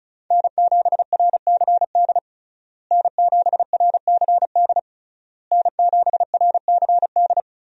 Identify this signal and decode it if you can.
Morse code, heard as a keyed tone, about 34 wpm, 700 Hz. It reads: N7RCD N7RCD N7RCD
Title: USA Call Signs - 5 Characters - Repeated 3x - 35wpm